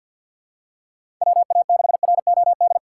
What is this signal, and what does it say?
Morse code, heard as a keyed tone, about 53 wpm, 700 Hz. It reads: WA6ROD
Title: USA Call Signs - 6 Characters - 6x Character Spacing - 50wpm